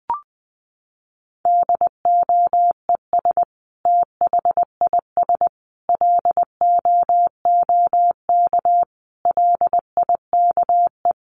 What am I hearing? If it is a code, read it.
Morse code, heard as a keyed tone, about 20 wpm, 700 Hz. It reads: DOES THIS LOOK LIKE
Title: Sentences from Top 1500 Words 20wpm